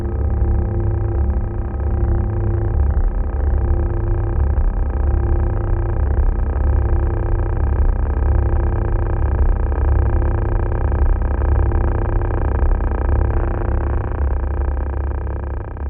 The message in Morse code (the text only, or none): none